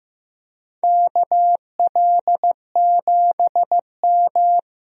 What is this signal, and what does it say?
Morse code, heard as a keyed tone, about 15 wpm, 700 Hz. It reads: KL7M
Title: DXCC-100 Call Signs - 4 Characters - 15wpm